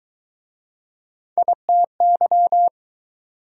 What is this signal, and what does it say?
Morse code, heard as a keyed tone, about 21 wpm, 700 Hz. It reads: ITY
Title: Speed-Racing Prefixes and Suffixes 17wpm